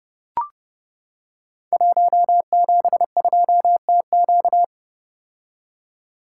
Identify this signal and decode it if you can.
Morse code, heard as a keyed tone, about 30 wpm, 700 Hz. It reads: 172TQ
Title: Code Groups - 5 Letter:Numbers - 30wpm